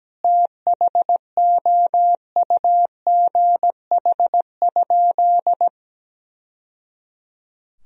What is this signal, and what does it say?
Morse code, heard as a keyed tone, about 17 wpm, 700 Hz. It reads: THOUGH?
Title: Sentences from Top 1000 Words - Review Part 2 - 17wpm